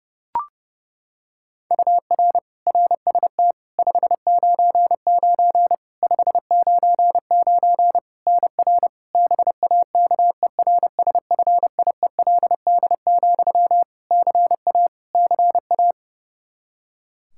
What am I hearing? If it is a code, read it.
Morse code, heard as a keyed tone, about 30 wpm, 700 Hz. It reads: UR RST 599 599 NR BAKERSFIELD, CA CA
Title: Practice QSO - Exchange 1 - Beginning - 30wpm